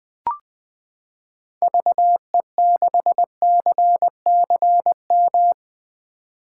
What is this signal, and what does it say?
Morse code, heard as a keyed tone, about 20 wpm, 700 Hz. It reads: VE6CCM
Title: 2023 SOTA Chaser Call Signs - 20wpm